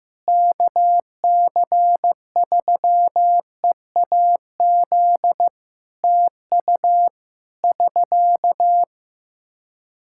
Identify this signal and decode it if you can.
Morse code, heard as a keyed tone, about 15 wpm, 700 Hz. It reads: KC3EAZ TU <SK>
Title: Practice QSO - Exchange 4 - Ending - 15wpm